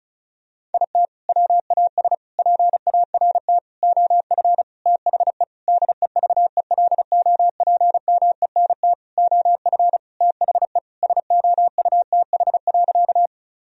Morse code, heard as a keyed tone, about 34 wpm, 700 Hz. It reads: IT WAS PART OF THE DEVELOPMENT OF THE SOUTH.
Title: Sentences from Top 1000 Words - Review Part 4 - 35wpm